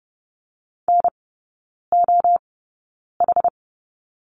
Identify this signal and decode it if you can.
Morse code, heard as a keyed tone, about 30 wpm, spaced as for 9 wpm, 700 Hz. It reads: NOH
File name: 081 Lesson 14e - Number 2 - Three Characters 30wpm